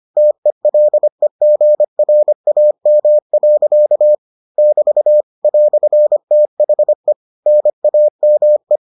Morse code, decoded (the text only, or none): TELEGRAM. <BT> "THE NAME